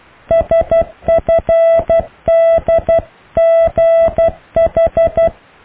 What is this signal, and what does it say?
Morse code, heard as a keyed tone, about 12 wpm, 660 Hz. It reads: SFDGH
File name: cw / Alle letters